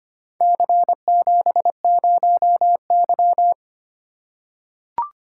Morse code, heard as a keyed tone, about 25 wpm, 700 Hz. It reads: C70Y
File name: practice / 25wpm / Four Letter-Number Combinations - 25wpm